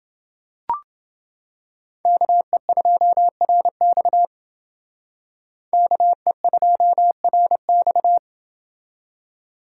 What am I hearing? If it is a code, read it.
Morse code, heard as a keyed tone, about 28 wpm, 700 Hz. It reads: KE2RX KE2RX